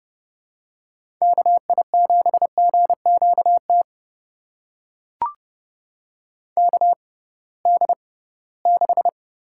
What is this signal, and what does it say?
Morse code, heard as a keyed tone, about 30 wpm, spaced as for 19 wpm, 700 Hz. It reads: KI7GQT KD6
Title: USA Call Signs - 6 Characters - 6x Character Spacing - 30wpm